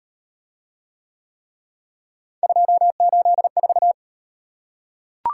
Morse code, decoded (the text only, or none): 284